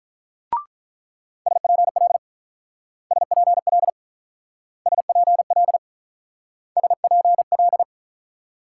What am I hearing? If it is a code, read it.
Morse code, heard as a keyed tone, about 45 wpm, 700 Hz. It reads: SPL SPL SPL SPL